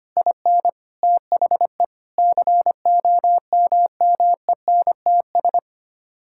Morse code, heard as a keyed tone, about 25 wpm, 700 Hz. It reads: IN THE COMMENTS
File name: Sets of 9 Words - 25wpm